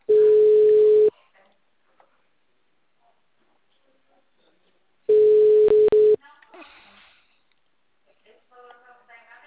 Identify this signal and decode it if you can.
Ringback tone (Europe, 425 Hz), with speech.
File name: out- -2014- - - .30369